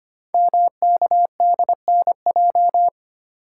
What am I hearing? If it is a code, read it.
Morse code, heard as a keyed tone, about 25 wpm, 700 Hz. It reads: MKDNJ
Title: Code Groups - 5 Letters - 25wpm